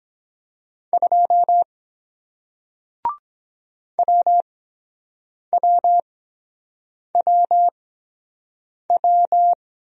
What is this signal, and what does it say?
Morse code, heard as a keyed tone, about 23 wpm, spaced as for 7 wpm, 700 Hz. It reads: 2 WWWW